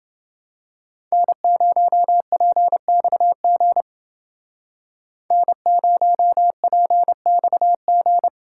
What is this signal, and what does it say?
Morse code, heard as a keyed tone, about 31 wpm, 700 Hz. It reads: N0PXG N0PXG